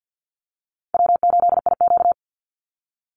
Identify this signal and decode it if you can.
Morse code, heard as a keyed tone, about 48 wpm, 700 Hz. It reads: W8IQ
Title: Easy Call Signs Repeated 3x - 50wpm